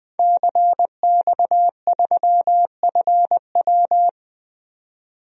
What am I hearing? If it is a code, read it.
Morse code, heard as a keyed tone, about 21 wpm, 700 Hz. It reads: CX3FW
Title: DXCC-100 Call Signs - 5 Characters - 20wpm